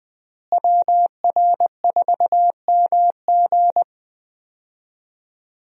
Morse code, heard as a keyed tone, about 20 wpm, 700 Hz. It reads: WR4MG
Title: USA Call Signs - 5 Characters - Repeated 3x - 20wpm